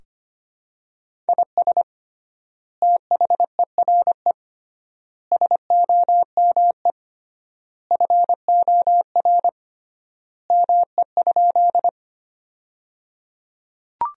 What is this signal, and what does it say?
Morse code, heard as a keyed tone, about 25 wpm, 700 Hz. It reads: IS THERE SOME FOR ME?
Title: 3x Word-Spacing - Sentences from Top 200 Words - 25wpm